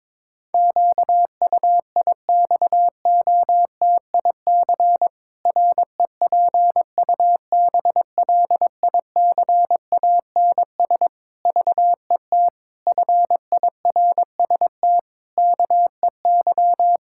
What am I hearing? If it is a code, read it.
Morse code, heard as a keyed tone, about 22 wpm, 700 Hz. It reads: QUIXOTIC REPUBLICANS VET FIRST KEY
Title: Pangram Challenge v2 - 22wpm